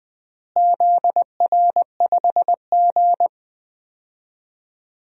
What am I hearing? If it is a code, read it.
Morse code, heard as a keyed tone, about 21 wpm, 700 Hz. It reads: ZR5G
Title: DXCC-100 Call Signs - 4 Characters - Repeated 3x - 20wpm